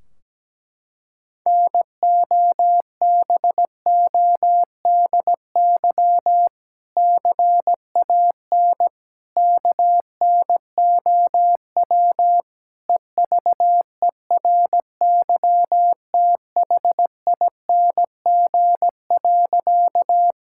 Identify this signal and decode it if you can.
Morse code, heard as a keyed tone, about 17 wpm, 700 Hz. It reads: NOBODY CAN KNOW EVERYTHING.
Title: Sentences from Top 1600 Words 17wpm